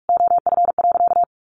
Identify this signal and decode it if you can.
Morse code, heard as a keyed tone, about 44 wpm, 700 Hz. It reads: OF.
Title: Sentences from Top 1300 Words 45wpm